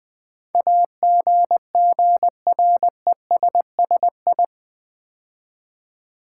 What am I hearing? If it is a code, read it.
Morse code, heard as a keyed tone, about 20 wpm, 700 Hz. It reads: AGGRESSI
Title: Build A Word - 4 Letters At A Time - 10-Letter Words - 20wpm